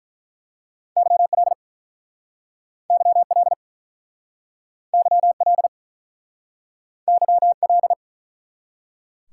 Morse code, heard as a keyed tone, about 48 wpm, 700 Hz. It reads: YL YL YL YL